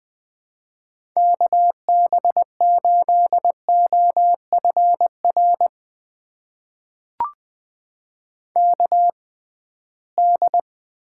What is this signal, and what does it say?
Morse code, heard as a keyed tone, about 20 wpm, spaced as for 15 wpm, 700 Hz. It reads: KB8OFR KD
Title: USA Call Signs - 6 Characters - 6x Character Spacing - 20wpm